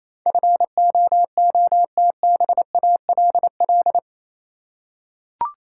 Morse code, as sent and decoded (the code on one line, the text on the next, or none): ..-. --- --- - -... .- .-.. .-..
FOOTBALL